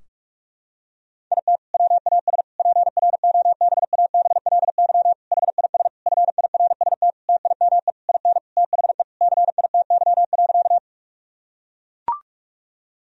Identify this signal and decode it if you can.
Morse code, heard as a keyed tone, about 42 wpm, 700 Hz. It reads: IT WAS PROBABLY HIS FIRST TIME IN THE CITY.
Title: Sentences from Top 1000 Words - Review Part 4 - 45wpm